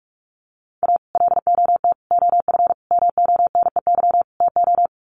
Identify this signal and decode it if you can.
Morse code, heard as a keyed tone, about 44 wpm, 700 Hz. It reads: A LOT OF MONEY TO
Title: Sets of 5 Words Encore2 45wpm